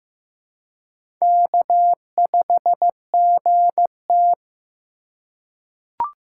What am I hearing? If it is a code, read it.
Morse code, heard as a keyed tone, about 15 wpm, 700 Hz. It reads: K5GT
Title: USA Call Signs - 4 Characters - 3x Character Spacing - 15wpm